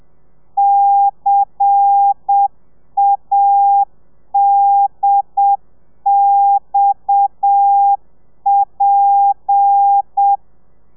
Morse code, heard as a keyed tone, about 7 wpm, 800 Hz. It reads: CADXP
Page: Lezione 1: audio a 30 CPM